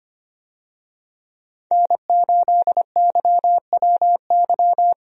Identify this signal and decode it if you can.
Morse code, heard as a keyed tone, about 25 wpm, 700 Hz. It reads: N8YWY
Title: USA Call Signs - 5 Characters - Repeated 3x - 25wpm